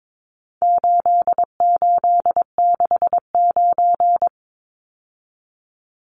Morse code, heard as a keyed tone, about 22 wpm, 700 Hz. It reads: 8869
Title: Numbers - 4-Digit 22wpm